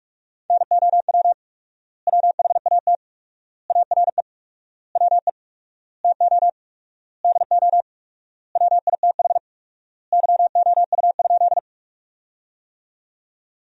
Morse code, heard as a keyed tone, about 46 wpm, 700 Hz. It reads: NOW WHAT ARE WE TO DO WITH YOU?
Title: 4x Word-Spacing - Sentences from Top 100 Words - 45wpm